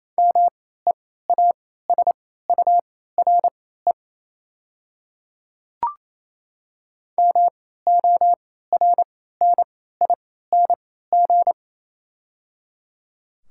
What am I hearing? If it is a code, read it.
Morse code, heard as a keyed tone, about 27 wpm, spaced as for 16 wpm, 700 Hz. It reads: MEASURE MORNING
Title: 3x Character Spacing - Top 500 Words - 28wpm